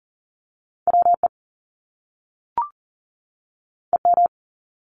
Morse code, heard as a keyed tone, about 40 wpm, 700 Hz. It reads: WE EM
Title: Top Two Letter Combinations in Words 40wpm